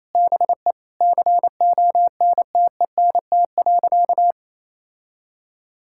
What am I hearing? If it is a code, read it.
Morse code, heard as a keyed tone, about 28 wpm, 700 Hz. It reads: BE CONTENT.
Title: Sentences from Top 1000 Words - Review Part 6 - 28wpm